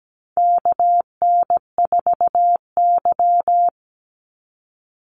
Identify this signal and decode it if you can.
Morse code, heard as a keyed tone, about 17 wpm, 700 Hz. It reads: KN4Y
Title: Call Signs Repeated 3x - 17wpm